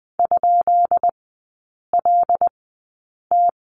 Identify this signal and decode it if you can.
Morse code, heard as a keyed tone, about 20 wpm, spaced as for 8 wpm, 700 Hz. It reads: ?LT
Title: 191 Lesson 31e - Letter v - Three Characters 20wpm